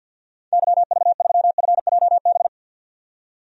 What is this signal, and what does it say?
Morse code, heard as a keyed tone, about 53 wpm, 700 Hz. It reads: YV3FJB
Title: DXCC-100 Call Signs - 6 Characters - 50wpm